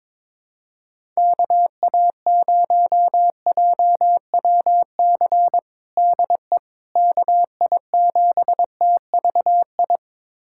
Morse code, heard as a keyed tone, about 22 wpm, 700 Hz. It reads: KA0JWC DE KI7TVI
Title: USA Callsign DE Callsign - 6 Characters - 22wpm